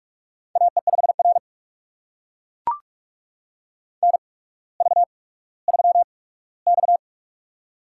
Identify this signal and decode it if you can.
Morse code, heard as a keyed tone, about 44 wpm, spaced as for 19 wpm, 700 Hz. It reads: AE5R NV3X
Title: USA Call Signs - 4 Characters - 8x Character Spacing - 45wpm